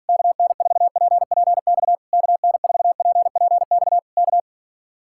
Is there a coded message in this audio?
Yes: Morse code from 0.1 to 4.4 s, heard as a keyed tone, about 48 wpm, 700 Hz. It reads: KN4PPX KN4PPX K